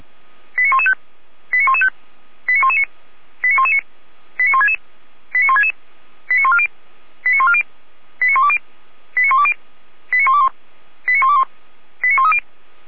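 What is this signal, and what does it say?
Five-tone selective call (ZVEI), heard as quick runs of five tones, tones of about 70 ms. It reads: 89115 89115 89119 89119 89166 89166 89140 89140 89120 89120 89121 89121 89130
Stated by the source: ZVEI five-tone call 89050 received on an SDR.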